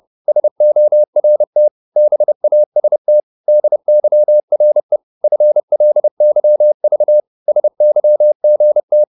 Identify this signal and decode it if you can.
Morse code, heard as a keyed tone, about 30 wpm, 600 Hz. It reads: SORT BAST DYRE FLYV SYGT
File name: Korte ord DK 30wpm